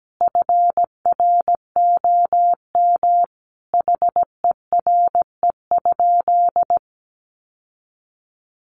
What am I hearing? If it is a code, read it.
Morse code, heard as a keyed tone, about 17 wpm, 700 Hz. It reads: FROM HERE?
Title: Sentences from Top 1100 Words 17wpm